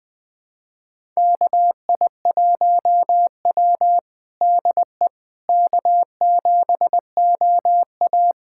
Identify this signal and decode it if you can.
Morse code, heard as a keyed tone, about 20 wpm, 700 Hz. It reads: KI1W DE K7OA
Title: USA Callsign DE Callsign - 4 Characters - 20wpm